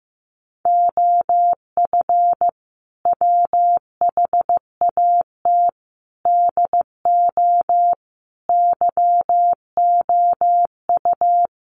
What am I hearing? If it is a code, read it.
Morse code, heard as a keyed tone, about 15 wpm, 700 Hz. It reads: OF WHAT DO YOU
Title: Sentences from Top 1300 Words 15wpm